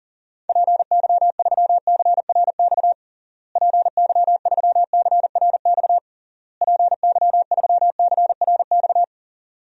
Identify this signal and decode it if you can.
Morse code, heard as a keyed tone, about 40 wpm, 700 Hz. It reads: PY3CRX PY3CRX PY3CRX
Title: DXCC-100 Call Signs - 6 Characters - Repeated 3x - 40wpm